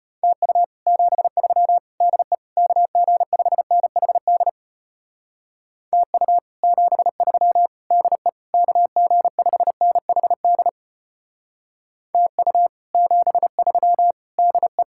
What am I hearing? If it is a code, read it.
Morse code, heard as a keyed tone, about 33 wpm, 700 Hz. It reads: TU 73 DE KG5NHD TU 73 DE KG5NHD TU 73 DE